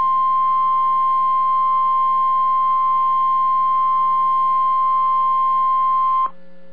Emergency Alert System (SAME) tones (the 1050 Hz attention tone). It is heard from 0.6 to 6.3 s.